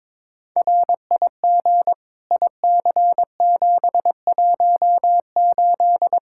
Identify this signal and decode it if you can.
Morse code, heard as a keyed tone, about 22 wpm, 700 Hz. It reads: RIG IC718
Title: Practice QSO - Exchange 2 - Info - 22wpm